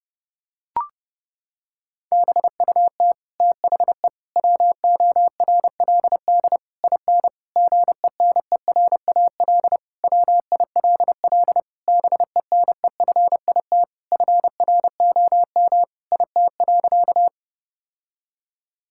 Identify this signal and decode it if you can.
Morse code, heard as a keyed tone, about 32 wpm, 700 Hz. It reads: BUT THE WORLD IN GENERAL WILL BENEFIT FROM IT.
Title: Sentences from Top 1000 Words - Review Part 6 - 30wpm